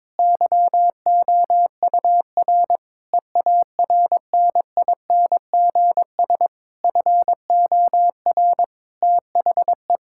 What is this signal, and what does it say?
Morse code, heard as a keyed tone, about 22 wpm, 700 Hz. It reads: YOUR EARNINGS FOR THE
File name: Difficult Sentences with Hints - 22wpm